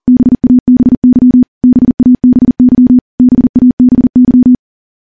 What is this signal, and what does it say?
Morse code, heard as a keyed tone, about 43 wpm, 250 Hz. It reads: BABY BABY BABY
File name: morse / morse-training-40WPM